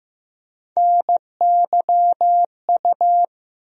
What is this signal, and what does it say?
Morse code, heard as a keyed tone, about 15 wpm, 700 Hz. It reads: NYU
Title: USA Call Sign Postfixes - 3 Letters 15wpm